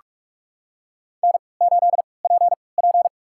Morse code, heard as a keyed tone, about 44 wpm, spaced as for 25 wpm, 700 Hz. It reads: N8PP
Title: USA Call Signs - 3x Character Spacing - 45wpm